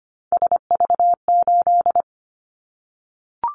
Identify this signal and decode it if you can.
Morse code, heard as a keyed tone, about 25 wpm, 700 Hz. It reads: SV8
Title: Callsign Prefixes for Top 50 DXCC - 25wpm